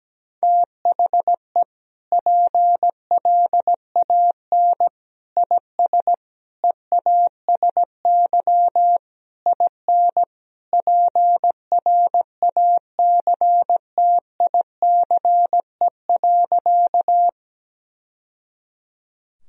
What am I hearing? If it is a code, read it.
Morse code, heard as a keyed tone, about 17 wpm, 700 Hz. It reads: THE PLAN IS EASY IN PRACTICE.
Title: Sentences from Top 1000 Words - Review Part 4 - 17wpm